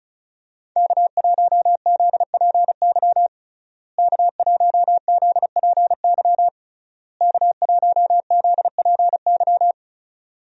Morse code, heard as a keyed tone, about 36 wpm, 700 Hz. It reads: K1ZPY K1ZPY K1ZPY
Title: USA Call Signs - 5 Characters - Repeated 3x - 35wpm